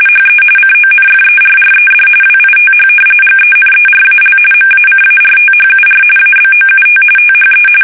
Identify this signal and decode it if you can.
RTTY, about 45.45 baud. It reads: HE CONFERRED WITH GENERAL MACARTHUR ABOUT THE
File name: fsk850 / battleshipnj